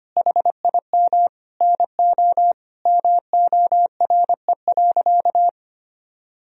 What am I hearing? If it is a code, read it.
Morse code, heard as a keyed tone, about 25 wpm, 700 Hz. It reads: HIM NO MORE.
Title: Sentences from Top 1000 Words - Review Part 1 - 25wpm